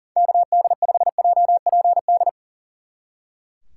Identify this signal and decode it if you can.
Morse code, heard as a keyed tone, about 38 wpm, 700 Hz. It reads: KD5JPD
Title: Speed Racing - USA 2x3 Call Signs 40wpm